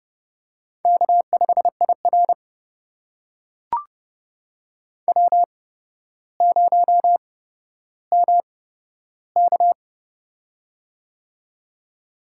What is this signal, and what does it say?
Morse code, heard as a keyed tone, about 31 wpm, spaced as for 13 wpm, 700 Hz. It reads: K5IR W0MK
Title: USA Call Signs - 4 Characters - 8x Character Spacing - 30wpm